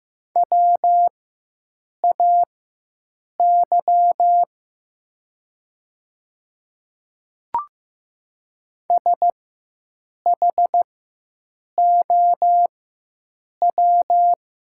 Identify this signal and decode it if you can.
Morse code, heard as a keyed tone, about 15 wpm, spaced as for 7 wpm, 700 Hz. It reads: WAY SHOW
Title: 4x Character Spacing - Top 200 Words - 15wpm